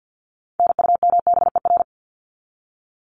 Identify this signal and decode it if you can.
Morse code, heard as a keyed tone, about 50 wpm, 700 Hz. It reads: NUMBER
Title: ES:S Suffix - Top 100 Words with Base Words 50wpm